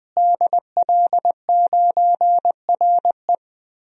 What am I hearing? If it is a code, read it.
Morse code, heard as a keyed tone, about 20 wpm, 700 Hz. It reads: DL9RE
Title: DXCC-100 Call Signs - 5 Characters - Repeated 2x - 20wpm